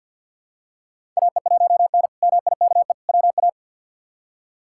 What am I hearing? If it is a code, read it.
Morse code, heard as a keyed tone, about 50 wpm, 700 Hz. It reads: AE1N MIKE WA
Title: K1USN SST QSO Practice - 50wpm